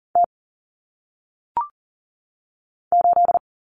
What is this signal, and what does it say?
Morse code, heard as a keyed tone, about 40 wpm, 700 Hz. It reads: T 8
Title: Common Cut Numbers - 1-Digit 40wpm